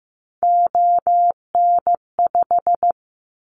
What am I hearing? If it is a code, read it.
Morse code, heard as a keyed tone, about 15 wpm, 700 Hz. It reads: ON5
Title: Callsign Prefixes for Top 100 DXCC - 15wpm